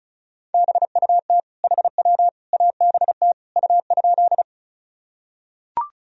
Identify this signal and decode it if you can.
Morse code, heard as a keyed tone, about 36 wpm, 700 Hz. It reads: BUT HW ABT U?
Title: Sentences from Top 100 Words and Abbreviations 35wpm